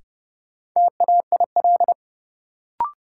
Morse code, heard as a keyed tone, about 31 wpm, 700 Hz. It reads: TAIL
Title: Sentences from Top 1000 Words - Review Part 3 - New Word Drill - 30wpm